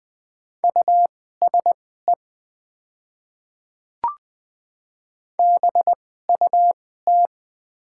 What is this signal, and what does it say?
Morse code, heard as a keyed tone, about 20 wpm, spaced as for 14 wpm, 700 Hz. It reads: USE BUT
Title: 2x Character Spacing - Top 100 Words - 20wpm